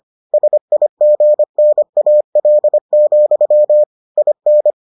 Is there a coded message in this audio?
Yes: Morse code from 0.3 to 4.7 s, heard as a keyed tone, about 25 wpm, 600 Hz. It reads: SIGNAL, IN